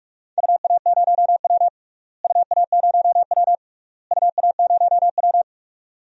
Morse code, heard as a keyed tone, about 41 wpm, 700 Hz. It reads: UA0W UA0W UA0W
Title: DXCC-100 Call Signs - 4 Characters - Repeated 3x - 45wpm